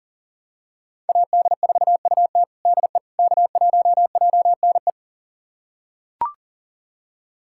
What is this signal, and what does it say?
Morse code, heard as a keyed tone, about 40 wpm, 700 Hz. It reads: AD4UT DE K1JNE
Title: USA Callsign DE Callsign - 5 Characters - 40wpm